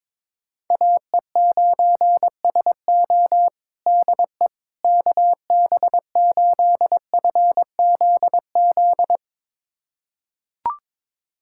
Morse code, heard as a keyed tone, about 22 wpm, 700 Hz. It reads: AE9SO DE KB8FZZ
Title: USA Callsign DE Callsign 22wpm